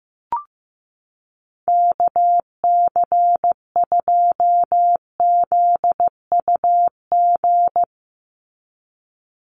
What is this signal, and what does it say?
Morse code, heard as a keyed tone, about 15 wpm, 700 Hz. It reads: KC2ZUG
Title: USA Call Signs 15wpm